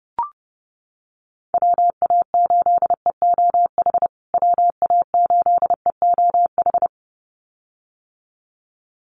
Morse code, heard as a keyed tone, about 30 wpm, 700 Hz. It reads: WA8EOH WA8EOH
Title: USA Call Signs Repeated 2x 30wpm